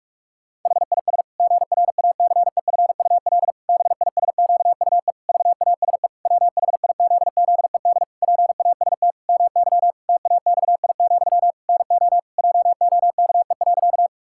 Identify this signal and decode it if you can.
Morse code, heard as a keyed tone, about 46 wpm, 700 Hz. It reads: HIS GRACEFUL BISQUE VASE WHIZZED PAST MY TAXI, NO JOKE.
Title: Pangram Challenge v2 - 45wpm